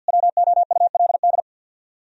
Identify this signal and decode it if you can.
Morse code, heard as a keyed tone, about 63 wpm, 700 Hz. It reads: WOULD